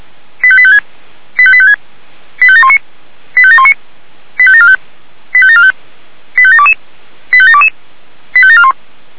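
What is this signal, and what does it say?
Five-tone selective call (ZVEI), heard as quick runs of five tones, tones of about 70 ms. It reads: 85856 85856 85719 85719 85745 85745 85722 85722 85741
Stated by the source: ZVEI five-tone selective call received on SDR.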